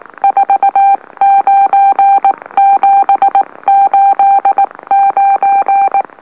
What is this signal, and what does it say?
Morse code, heard as a keyed tone, about 19 wpm, spaced as for 17 wpm, 790 Hz. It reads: 49789
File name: M04- -2345utc-LOLO